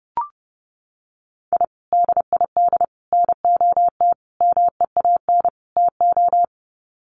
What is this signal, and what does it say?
Morse code, heard as a keyed tone, about 30 wpm, 700 Hz. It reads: I DID NOT MEAN TO
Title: Sets of 5 Words 30wpm